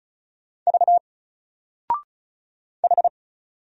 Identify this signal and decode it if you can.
Morse code, heard as a keyed tone, about 33 wpm, 700 Hz. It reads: V H
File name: Single-Letter-Number-35wpm